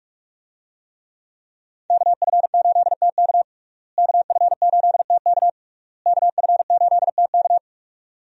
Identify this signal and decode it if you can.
Morse code, heard as a keyed tone, about 41 wpm, 700 Hz. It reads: KF8TK KF8TK KF8TK
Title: USA Call Signs - 5 Characters - Repeated 3x - 45wpm